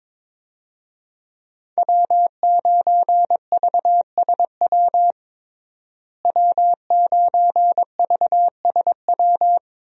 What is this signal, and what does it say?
Morse code, heard as a keyed tone, about 21 wpm, 700 Hz. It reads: W9VSW W9VSW
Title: USA Call Signs - 5 Characters - Repeated 3x - 22wpm